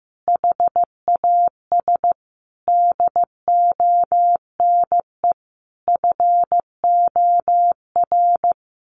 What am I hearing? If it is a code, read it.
Morse code, heard as a keyed tone, about 15 wpm, 700 Hz. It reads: HAS DONE FOR
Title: Sentences from Top 1400 Words 15wpm